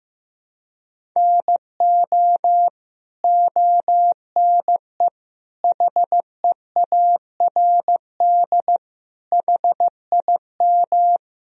Morse code, heard as a keyed tone, about 15 wpm, 700 Hz. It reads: NO ONE HEARD HIM
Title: Sentences from Top 1000 Words - Review Part 1 - 15wpm